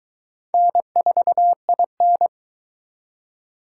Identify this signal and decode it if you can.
Morse code, heard as a keyed tone, about 22 wpm, 700 Hz. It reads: N4IN